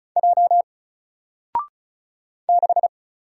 Morse code, heard as a keyed tone, about 34 wpm, 700 Hz. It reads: J 6
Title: Single Letter-Number Alt 35wpm